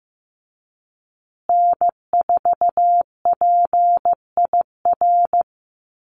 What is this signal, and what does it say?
Morse code, heard as a keyed tone, about 15 wpm, 700 Hz. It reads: N4PIR
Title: Call Signs Repeated 2x - 15wpm